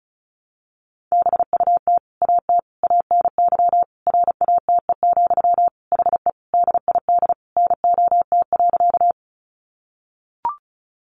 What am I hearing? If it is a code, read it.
Morse code, heard as a keyed tone, about 34 wpm, 700 Hz. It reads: BUT AT ANY RATE, HE DID NOT.
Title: Sentences from Top 700 Words 35wpm